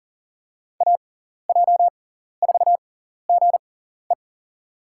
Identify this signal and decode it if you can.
Morse code, heard as a keyed tone, about 40 wpm, spaced as for 14 wpm, 700 Hz. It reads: AJ4GE
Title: USA Call Signs - 5 Characters - 6x Character Spacing - 40wpm